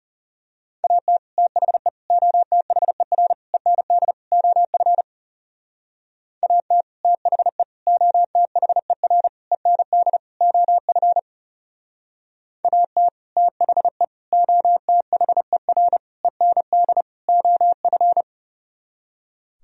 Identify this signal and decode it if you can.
Morse code, heard as a keyed tone, about 34 wpm, 700 Hz. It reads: AT THE OTHER END OF AT THE OTHER END OF AT THE OTHER END OF
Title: Speed-Racing - Sets of 5 Words 30wpm